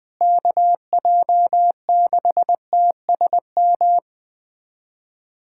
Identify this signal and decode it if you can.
Morse code, heard as a keyed tone, about 21 wpm, 700 Hz. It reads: KJ6TSM
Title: USA Callsign DE Callsign - 6 Characters - 20wpm